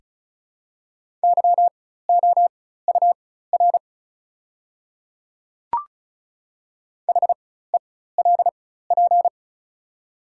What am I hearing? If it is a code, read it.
Morse code, heard as a keyed tone, about 36 wpm, spaced as for 16 wpm, 690 Hz. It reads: YOUR HELP
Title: 4x Character Spacing - Top 500 Words - 35wpm